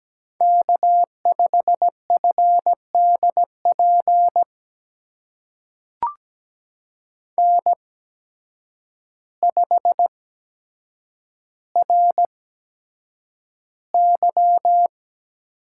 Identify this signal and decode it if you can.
Morse code, heard as a keyed tone, about 17 wpm, spaced as for 8 wpm, 700 Hz. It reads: K5FDP N5RY
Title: USA Call Signs - 5 Characters - 8x Character Spacing - 17wpm